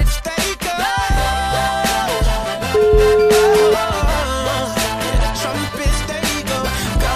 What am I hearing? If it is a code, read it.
Ringback tone (Europe, 425 Hz), with noise.